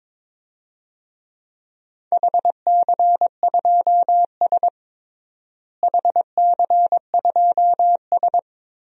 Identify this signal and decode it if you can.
Morse code, heard as a keyed tone, about 21 wpm, 700 Hz. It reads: HC2S HC2S
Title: DXCC-100 Call Signs - 4 Characters - Repeated 3x - 22wpm